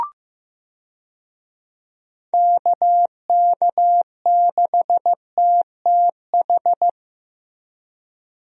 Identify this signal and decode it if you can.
Morse code, heard as a keyed tone, about 15 wpm, 700 Hz. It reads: KK6TTH
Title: USA Call Signs - 6 Characters - Repeated 2x - 15wpm